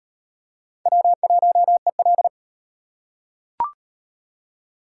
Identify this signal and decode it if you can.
Morse code, heard as a keyed tone, about 35 wpm, 700 Hz. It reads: W1EL